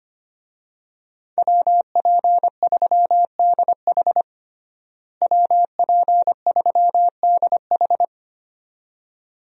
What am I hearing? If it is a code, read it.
Morse code, heard as a keyed tone, about 25 wpm, 700 Hz. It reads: WP3DH WP3DH
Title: USA Call Signs - 5 Characters - Repeated 2x - 25wpm